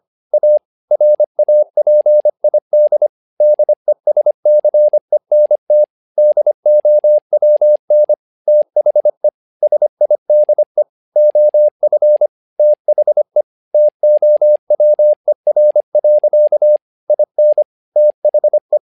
Morse code, heard as a keyed tone, about 25 wpm, 600 Hz. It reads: A RAPID DESCENT DOWN THE SIDE OF THE TOWER. IN THE
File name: learning center / tower trouble / tower trouble 25wpm